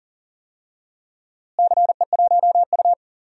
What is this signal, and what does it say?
Morse code, heard as a keyed tone, about 40 wpm, 700 Hz. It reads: CE1U
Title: DXCC-100 Call Signs - 4 Characters - Repeated 3x - 40wpm